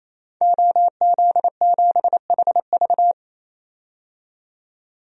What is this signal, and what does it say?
Morse code, heard as a keyed tone, about 28 wpm, 700 Hz. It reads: OZ7HV
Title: DXCC-100 Call Signs - 5 Characters - 28wpm